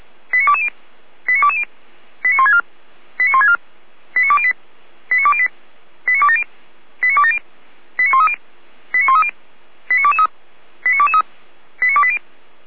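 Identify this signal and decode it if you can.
Five-tone selective call (ZVEI), heard as quick runs of five tones, tones of about 80 ms. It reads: 89220 89220 89164 89164 89208 89208 89270 89270 89130 89130 89223 89223 89280
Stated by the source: ZVEI five-tone call received on SDR.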